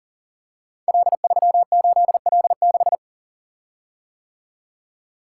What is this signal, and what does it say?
Morse code, heard as a keyed tone, about 41 wpm, 700 Hz. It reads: L38L6
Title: Code Groups - 5 Letter:Numbers - 40wpm